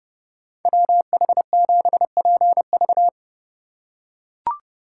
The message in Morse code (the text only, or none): WH7PV